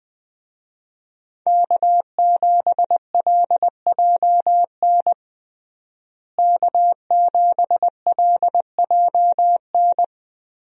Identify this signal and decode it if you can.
Morse code, heard as a keyed tone, about 20 wpm, 700 Hz. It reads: K7LJN K7LJN
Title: USA Call Signs - 5 Characters - Repeated 2x - 20wpm